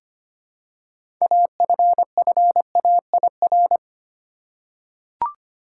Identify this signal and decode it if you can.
Morse code, heard as a keyed tone, about 25 wpm, 700 Hz. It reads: AFFAIR
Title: Six Letter Words - 25wpm